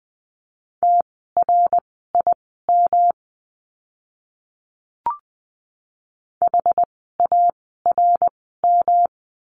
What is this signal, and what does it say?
Morse code, heard as a keyed tone, about 20 wpm, spaced as for 14 wpm, 700 Hz. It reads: TRIM HARM
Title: 108 Lesson 18f - Letter m - Words 20wpm